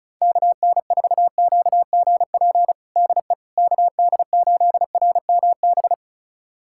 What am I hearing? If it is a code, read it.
Morse code, heard as a keyed tone, about 36 wpm, 700 Hz. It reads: KN4QGP DE KD8RMB
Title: USA Callsign DE Callsign 35wpm